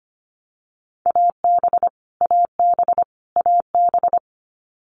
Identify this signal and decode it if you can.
Morse code, heard as a keyed tone, about 25 wpm, 700 Hz. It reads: AB AB AB
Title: Canadian Provinces and Territories - Repeated 3x - 25wpm